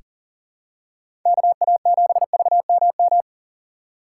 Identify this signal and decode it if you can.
Morse code, heard as a keyed tone, about 40 wpm, 700 Hz. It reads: KA7VMM
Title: USA Call Signs - 6 Characters - 2x Character Spacing - 40wpm